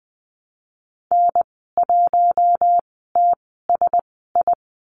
Morse code, heard as a keyed tone, about 20 wpm, spaced as for 14 wpm, 700 Hz. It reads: N1TSI
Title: 076 Lesson 13g - Letter L - Call Signs 20wpm